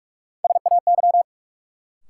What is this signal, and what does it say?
Morse code, heard as a keyed tone, about 46 wpm, 700 Hz. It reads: SAY
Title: Sentences from Top 200 Words - New Word Drill - 45wpm